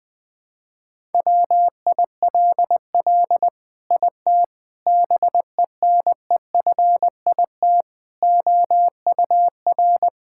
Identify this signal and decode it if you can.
Morse code, heard as a keyed tone, about 21 wpm, 700 Hz. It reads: WILL IT BENEFIT OUR
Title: Sentences from Top 1000 Words - Review Part 6 - 20wpm